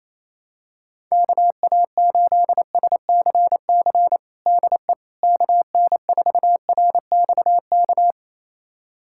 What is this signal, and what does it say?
Morse code, heard as a keyed tone, about 28 wpm, 700 Hz. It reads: KA8SCC DE KN4RXK
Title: USA Callsign DE Callsign 28wpm